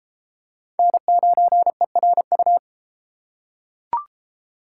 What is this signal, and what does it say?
Morse code, heard as a keyed tone, about 32 wpm, 700 Hz. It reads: N9ERU